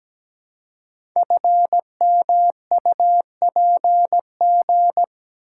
Morse code, heard as a keyed tone, about 17 wpm, 700 Hz. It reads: FMUPG
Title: Code Groups - 5 Letters - 17wpm